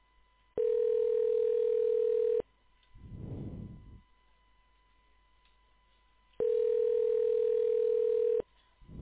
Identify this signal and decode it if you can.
Ringback tone (North America), 440 + 480 Hz.